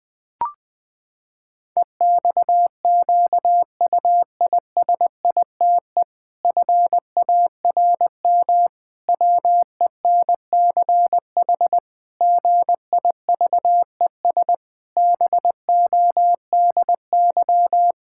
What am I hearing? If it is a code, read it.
Morse code, heard as a keyed tone, about 20 wpm, 700 Hz. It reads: EXQUISITE FARM WENCH GIVES BODY
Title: Pangram Challenge v2 - 20wpm